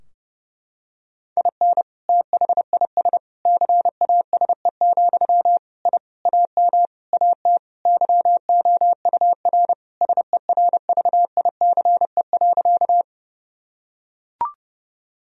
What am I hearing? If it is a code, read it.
Morse code, heard as a keyed tone, about 30 wpm, 700 Hz. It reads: IN THIS CASE, I AM AT YOUR SERVICE.
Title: Sentences from Top 1000 Words - Review Part 4 - 30wpm